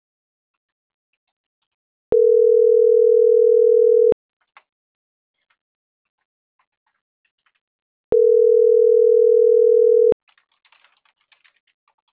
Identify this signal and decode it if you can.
Ringback tone (North America), 440 + 480 Hz.